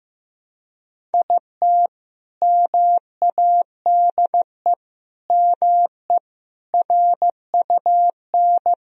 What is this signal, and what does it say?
Morse code, heard as a keyed tone, about 16 wpm, 700 Hz. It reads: IT MADE ME RUN
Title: Sentences from Top 1000 Words - Review Part 1 - 15wpm